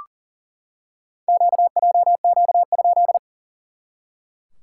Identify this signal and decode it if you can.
Morse code, heard as a keyed tone, about 40 wpm, 690 Hz. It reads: QJQ?
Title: SOTA QSO Practice 40wpm